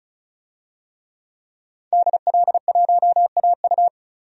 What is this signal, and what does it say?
Morse code, heard as a keyed tone, about 35 wpm, 700 Hz. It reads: DL1AU
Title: DXCC-100 Call Signs - 5 Characters - 35wpm